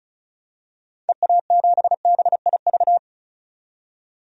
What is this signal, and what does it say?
Morse code, heard as a keyed tone, about 36 wpm, 700 Hz. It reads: EA7BIV
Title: DXCC-100 Call Signs - 6 Characters - 35wpm